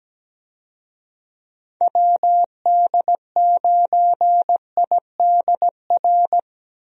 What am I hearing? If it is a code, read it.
Morse code, heard as a keyed tone, about 17 wpm, 700 Hz. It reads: WD9IDR
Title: USA Call Signs - 6 Characters - Repeated 2x - 17wpm